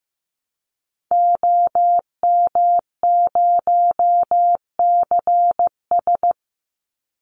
Morse code, heard as a keyed tone, about 15 wpm, 700 Hz. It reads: OM0CS
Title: Call Signs Repeated 2x - 15wpm